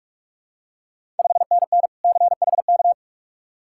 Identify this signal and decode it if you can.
Morse code, heard as a keyed tone, about 45 wpm, 700 Hz. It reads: 5NN CHK
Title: Ontario QSO Party Exchange 45wpm